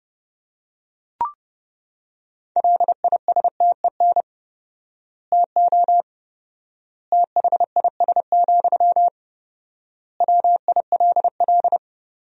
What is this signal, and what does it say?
Morse code, heard as a keyed tone, about 31 wpm, 700 Hz. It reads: LISTEN TO THIS, WILL
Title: 4x Word-Spacing - Sentences from Top 400 Words - 30wpm